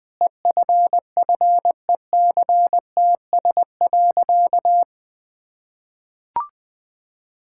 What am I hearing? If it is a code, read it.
Morse code, heard as a keyed tone, about 20 wpm, 700 Hz. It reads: EFFECTS.
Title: Sentences with 5-Word Phrases - 20wpm